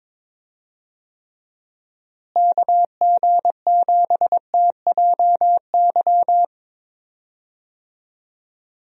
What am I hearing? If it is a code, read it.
Morse code, heard as a keyed tone, about 22 wpm, 700 Hz. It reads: KG7TJY
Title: USA Call Signs - 6 Characters - Repeated 3x - 22wpm